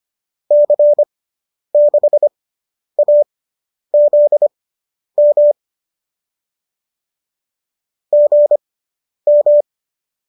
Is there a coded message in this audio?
Yes: Morse code from 0.5 to 9.6 s, heard as a keyed tone, about 25 wpm, spaced as for 10 wpm, 590 Hz. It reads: C6AZM GM